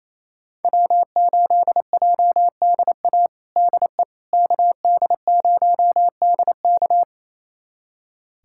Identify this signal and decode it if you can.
Morse code, heard as a keyed tone, about 28 wpm, 700 Hz. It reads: W8JDA DE KD0DK
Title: USA Callsign DE Callsign - 5 Characters - 28wpm